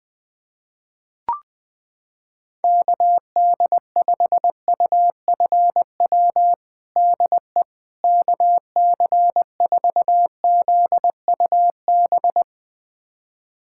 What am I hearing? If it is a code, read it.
Morse code, heard as a keyed tone, about 20 wpm, 700 Hz. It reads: KD5UFW DE KC4ZUB
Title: USA Callsign DE Callsign - 6 Characters - 20wpm